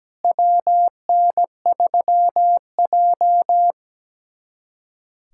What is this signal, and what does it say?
Morse code, heard as a keyed tone, about 17 wpm, 700 Hz. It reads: WN3J
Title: USA Callsign DE Callsign - 4 Characters - 17wpm